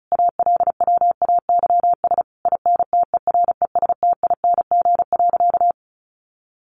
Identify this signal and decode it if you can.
Morse code, heard as a keyed tone, about 34 wpm, 700 Hz. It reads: ALWAYS INTERESTING.
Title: Sentences from Top 1200 Words 35wpm